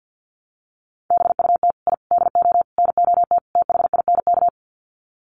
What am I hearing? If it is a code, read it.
Morse code, heard as a keyed tone, about 50 wpm, 700 Hz. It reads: BUT I DO NOT THINK
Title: Sets of 5 Words 50wpm